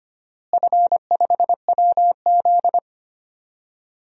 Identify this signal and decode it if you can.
Morse code, heard as a keyed tone, about 25 wpm, 700 Hz. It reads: F5WZ
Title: DXCC-100 Call Signs - 4 Characters - Repeated 3x - 25wpm